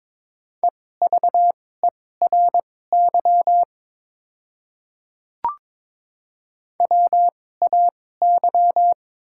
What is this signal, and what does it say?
Morse code, heard as a keyed tone, about 22 wpm, spaced as for 16 wpm, 700 Hz. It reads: EVERY WAY
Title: 2x Character Spacing - Top 200 Words - 22wpm